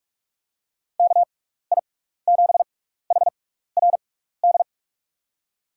Morse code, heard as a keyed tone, about 44 wpm, spaced as for 15 wpm, 700 Hz. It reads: KI7HRD
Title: USA Call Signs - 6 Characters - 6x Character Spacing - 45wpm